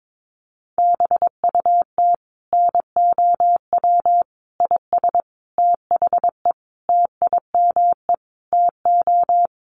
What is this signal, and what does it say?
Morse code, heard as a keyed tone, about 22 wpm, 700 Hz. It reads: BUT NOW IS THE TIME TO
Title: Sentences from Top 1500 Words 22wpm